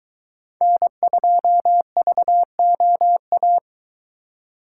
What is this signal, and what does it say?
Morse code, heard as a keyed tone, about 22 wpm, 700 Hz. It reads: N2VOA